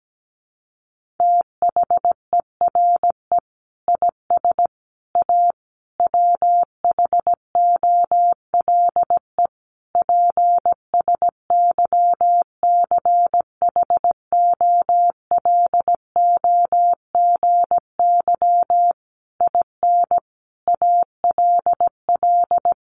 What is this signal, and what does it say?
Morse code, heard as a keyed tone, about 17 wpm, 700 Hz. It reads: THERE IS A WHOLE PSYCHOLOGY IN ALL
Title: Sentences from Top 1400 Words 17wpm